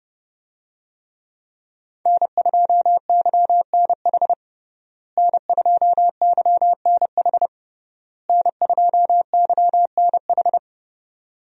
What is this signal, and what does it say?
Morse code, heard as a keyed tone, about 29 wpm, 700 Hz. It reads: N2YNH N2YNH N2YNH
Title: USA Call Signs - 5 Characters - Repeated 3x - 30wpm